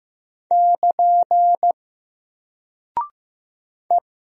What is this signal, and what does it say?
Morse code, heard as a keyed tone, about 15 wpm, 700 Hz. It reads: <KN> E
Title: Single Character (Letter:Number:Punctuation:Prosign) Alt 15wpm